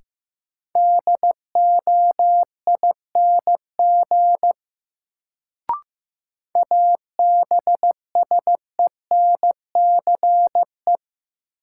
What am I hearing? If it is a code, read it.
Morse code, heard as a keyed tone, about 15 wpm, 700 Hz. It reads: DOING ABSENCE
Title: Sets of 4 Words - New Word Drill - 15wpm